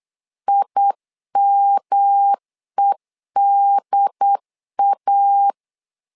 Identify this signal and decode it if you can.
Morse code, heard as a keyed tone, about 8 wpm, 800 Hz. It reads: IMEDA
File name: LEZIONI / 3 MWRO / 3 M 8wpm 2300